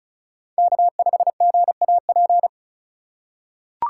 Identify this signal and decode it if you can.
Morse code, heard as a keyed tone, about 34 wpm, 700 Hz. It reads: K5GAP
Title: USA Call Signs - 2x Character Spacing - 35wpm